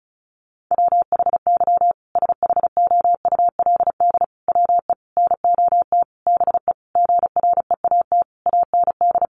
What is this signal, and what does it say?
Morse code, heard as a keyed tone, about 34 wpm, 700 Hz. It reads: WHY SHOULD WE NOT BE GREAT AND
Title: Sentences from Top 1500 Words 35wpm